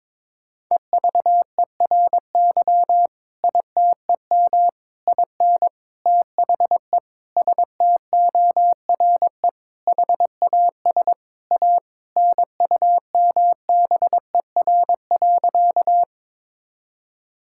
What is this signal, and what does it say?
Morse code, heard as a keyed tone, about 22 wpm, 700 Hz. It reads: EVERY ITEM IN THE STORE HAS A NUMBER.
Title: Sentences from Top 1000 Words - Review Part 6 - 22wpm